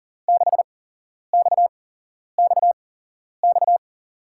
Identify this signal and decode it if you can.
Morse code, heard as a keyed tone, about 40 wpm, spaced as for 11 wpm, 700 Hz. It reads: 6XXX
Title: USA Call Signs - 5 Characters - 8x Character Spacing - 40wpm